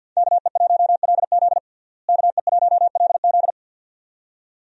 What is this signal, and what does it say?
Morse code, heard as a keyed tone, about 53 wpm, 700 Hz. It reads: KE1LZ KE1LZ
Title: USA Call Signs - 5 Characters - Repeated 2x - 50wpm